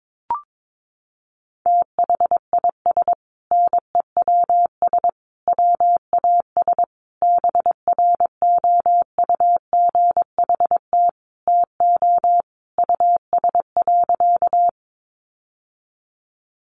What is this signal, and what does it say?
Morse code, heard as a keyed tone, about 22 wpm, 700 Hz. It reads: THIS NEWS WAS BROUGHT TO US.
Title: Sentences from Top 800 Words 22wpm